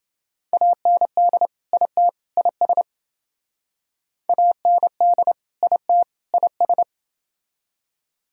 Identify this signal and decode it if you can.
Morse code, heard as a keyed tone, about 28 wpm, 700 Hz. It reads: AND IT IS AND IT IS